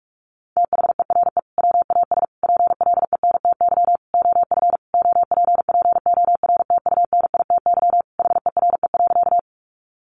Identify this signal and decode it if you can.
Morse code, heard as a keyed tone, about 44 wpm, 700 Hz. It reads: THERE WAS PLENTY OF OPPORTUNITY HERE.
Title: Sentences from Top 1300 Words 45wpm